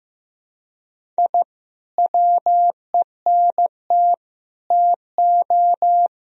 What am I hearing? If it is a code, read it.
Morse code, heard as a keyed tone, about 15 wpm, 700 Hz. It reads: I WENT TO
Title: Sentences from Top 1000 Words - Review Part 1 - 15wpm